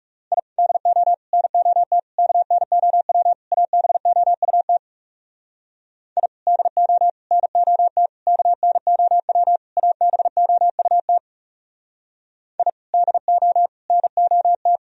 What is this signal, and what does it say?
Morse code, heard as a keyed tone, about 38 wpm, 700 Hz. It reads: I DO NOT KNOW ABOUT I DO NOT KNOW ABOUT I DO NOT